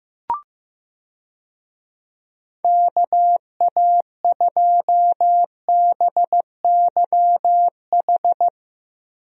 Morse code, heard as a keyed tone, about 15 wpm, 700 Hz. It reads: KA2BYH
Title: USA Call Signs - 6 Characters - Repeated 2x - 15wpm